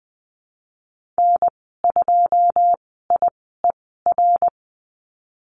Morse code, heard as a keyed tone, about 20 wpm, spaced as for 15 wpm, 700 Hz. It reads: N2IER
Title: 173 Lesson 28g - Number 9 - Call Signs 20wpm